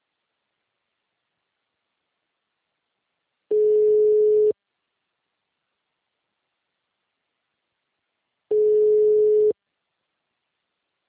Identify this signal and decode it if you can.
Ringback tone (Europe, 425 Hz).